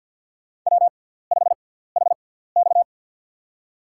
Morse code, heard as a keyed tone, about 51 wpm, spaced as for 17 wpm, 700 Hz. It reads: W5HX
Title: USA Call Signs - 6x Character Spacing - 50wpm